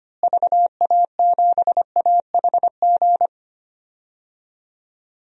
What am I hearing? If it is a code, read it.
Morse code, heard as a keyed tone, about 25 wpm, 700 Hz. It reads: VA7AHG
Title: DXCC-100 Call Signs - 6 Characters - Repeated 2x - 25wpm